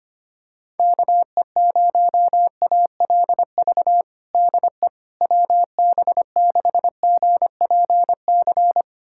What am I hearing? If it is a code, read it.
Morse code, heard as a keyed tone, about 25 wpm, 700 Hz. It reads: KE0ALV DE WB6GPC
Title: USA Callsign DE Callsign - 6 Characters - 25wpm